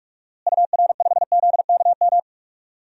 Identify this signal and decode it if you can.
Morse code, heard as a keyed tone, about 41 wpm, 700 Hz. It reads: UR5ZKM
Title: DXCC-100 Call Signs - 6 Characters - Repeated 3x - 45wpm